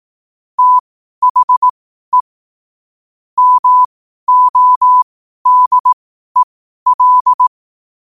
Morse code, heard as a keyed tone, about 18 wpm, spaced as for 12 wpm, 1000 Hz. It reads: THE MODEL
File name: qotdfiles / 12wpm